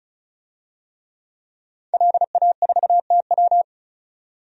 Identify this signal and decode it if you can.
Morse code, heard as a keyed tone, about 36 wpm, 700 Hz. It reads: LA4TW
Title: DXCC-100 Call Signs - 5 Characters - 35wpm